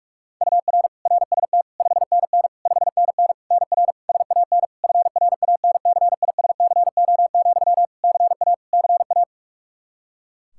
Morse code, heard as a keyed tone, about 41 wpm, 700 Hz. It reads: UR RST 5NN 5NN NR SAN FRANCISCO, CA CA
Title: Practice QSO - Exchange 1 - Beginning - 45wpm